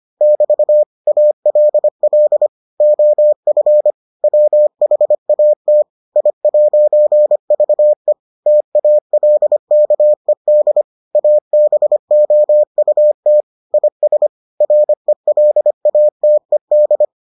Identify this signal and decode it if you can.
Morse code, heard as a keyed tone, about 25 wpm, 600 Hz. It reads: <BT> ALL OF WHAT I'VE TALKED ABOUT IS RELATED